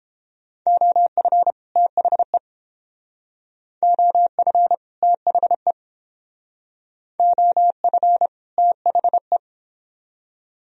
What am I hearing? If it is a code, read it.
Morse code, heard as a keyed tone, about 28 wpm, 700 Hz. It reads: OF THE OF THE OF THE